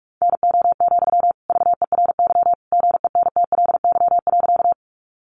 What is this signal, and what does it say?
Morse code, heard as a keyed tone, about 44 wpm, 700 Hz. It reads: NO, VERY GENTLY.
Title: Sentences from Top 1500 Words 45wpm